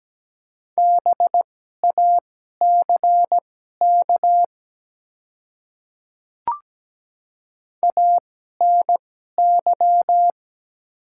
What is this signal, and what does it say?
Morse code, heard as a keyed tone, about 17 wpm, spaced as for 12 wpm, 700 Hz. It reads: BACK ANY
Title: 2x Character Spacing - Top 200 Words - 17wpm